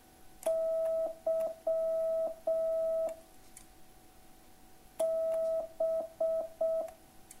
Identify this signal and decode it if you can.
Morse code, heard as a keyed tone, about 6 wpm, 650 Hz. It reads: Y B